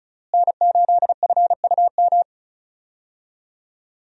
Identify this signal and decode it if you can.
Morse code, heard as a keyed tone, about 34 wpm, 700 Hz. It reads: N8FUM
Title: USA Call Signs - 5 Characters - Repeated 2x - 35wpm